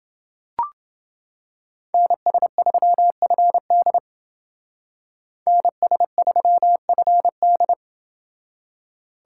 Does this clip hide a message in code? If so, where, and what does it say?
Yes: Morse code from 1.9 to 7.7 s, heard as a keyed tone, about 29 wpm, 700 Hz. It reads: NS3FD NS3FD